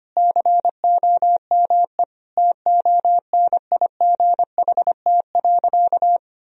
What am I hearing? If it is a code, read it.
Morse code, heard as a keyed tone, about 25 wpm, 700 Hz. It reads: COME TONIGHT.
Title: Sentences from Top 1000 Words - Review Part 3 - 25wpm